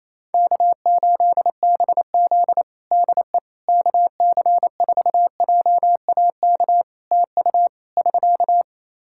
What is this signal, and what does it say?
Morse code, heard as a keyed tone, about 27 wpm, 700 Hz. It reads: K8BZ DE KC4JAK TU <SK>
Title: Practice QSO - Exchange 4 - Ending - 28wpm